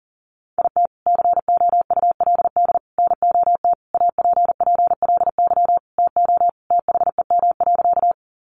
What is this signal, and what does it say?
Morse code, heard as a keyed tone, about 40 wpm, 700 Hz. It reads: IT COULD NOT APPLY TO THEM.
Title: Sentences from Top 1200 Words 40wpm